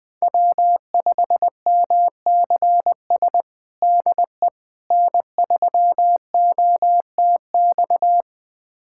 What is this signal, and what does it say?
Morse code, heard as a keyed tone, about 20 wpm, 700 Hz. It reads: W5MCS DE N3OTX
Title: USA Callsign DE Callsign - 5 Characters - 20wpm